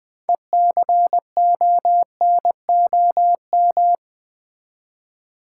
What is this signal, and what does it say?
Morse code, heard as a keyed tone, about 20 wpm, 700 Hz. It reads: ECONOM
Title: Build A Word - 3 Letters At A Time - 8-Letter Words - 20wpm